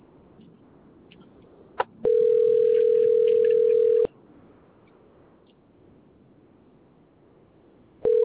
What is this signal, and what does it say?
Ringback tone (North America), 440 + 480 Hz.